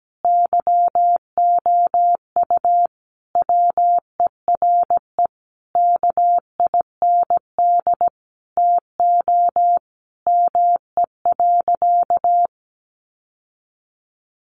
Morse code, heard as a keyed tone, about 17 wpm, 700 Hz. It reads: YOU WERE KIND TO ME.
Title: Sentences from Top 1100 Words 17wpm